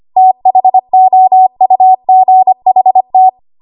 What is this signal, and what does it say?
Morse code, heard as a keyed tone, about 25 wpm, 750 Hz. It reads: THOUGHT
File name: Morse / 25WPM